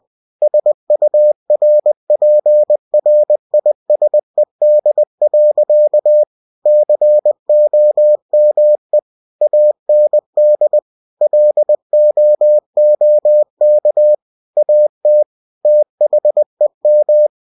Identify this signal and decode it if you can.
Morse code, heard as a keyed tone, about 20 wpm, 600 Hz. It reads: SURPRISED. COME AND LOOK AT THEM
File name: cw-books / winnie-the-pooh / winnie-20wpm-page-0031